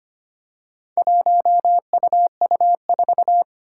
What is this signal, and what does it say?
Morse code, heard as a keyed tone, about 25 wpm, 700 Hz. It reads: 1UU4
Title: Four Challenging Letter-Number Combinations 25wpm